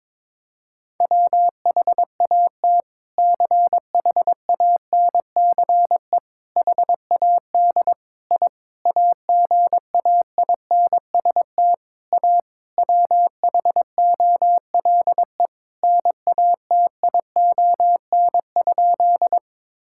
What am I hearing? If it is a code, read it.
Morse code, heard as a keyed tone, about 22 wpm, 700 Hz. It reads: WHAT CHANCE HAD I AGAINST A WHOLE NATION?
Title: Sentences from Top 1000 Words - Review Part 6 - 22wpm